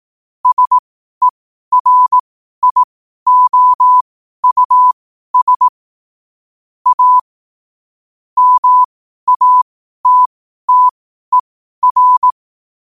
Morse code, heard as a keyed tone, about 18 wpm, spaced as for 12 wpm, 1000 Hz. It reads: SERIOUS A MATTER
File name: 12wpm